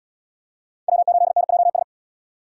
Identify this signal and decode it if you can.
Morse code, heard as a keyed tone, about 66 wpm, 700 Hz. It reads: W8NJS